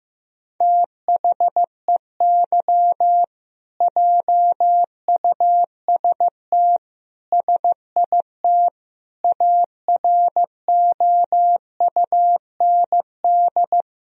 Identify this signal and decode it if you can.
Morse code, heard as a keyed tone, about 16 wpm, 700 Hz. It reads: THEY JUST SIT AROUND
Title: Sentences from Top 1000 Words - Review Part 3 - 15wpm